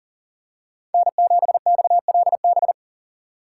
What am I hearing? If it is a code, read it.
Morse code, heard as a keyed tone, about 37 wpm, 700 Hz. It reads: N7XLB
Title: USA Call Signs - 5 Characters - 4x Character Spacing - 40wpm